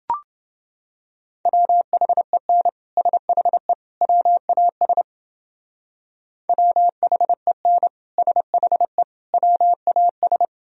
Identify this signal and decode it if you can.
Morse code, heard as a keyed tone, about 28 wpm, 700 Hz. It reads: WHEN SHE WAS WHEN SHE WAS